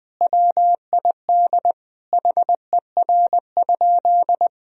Morse code, heard as a keyed tone, about 19 wpm, 700 Hz. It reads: WID HER?
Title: Sentences from Top 400 Words and Abbreviations 20wpm